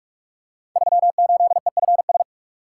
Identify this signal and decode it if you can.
Morse code, heard as a keyed tone, about 44 wpm, 700 Hz. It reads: 38EFS
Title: Code Groups - 5 Letter:Numbers - 45wpm